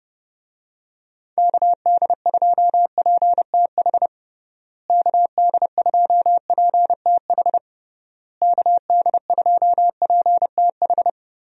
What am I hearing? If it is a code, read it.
Morse code, heard as a keyed tone, about 30 wpm, 700 Hz. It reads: KD2PTH KD2PTH KD2PTH
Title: USA Call Signs - 6 Characters - Repeated 3x - 30wpm